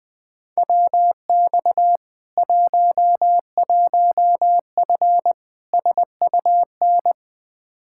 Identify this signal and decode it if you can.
Morse code, heard as a keyed tone, about 20 wpm, 700 Hz. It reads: WX 11F SUN
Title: Practice QSO - Exchange 2 - Info - 20wpm